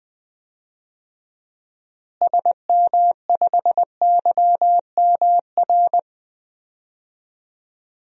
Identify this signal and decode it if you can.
Morse code, heard as a keyed tone, about 20 wpm, 700 Hz. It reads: SM5YMR
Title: DXCC-100 Call Signs - 6 Characters - Repeated 3x - 20wpm